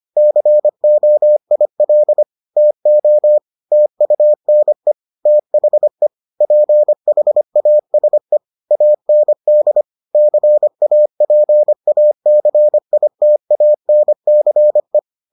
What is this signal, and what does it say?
Morse code, heard as a keyed tone, about 25 wpm, 600 Hz. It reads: COIL TO TUNE THE PHASE AND CAPACITANCE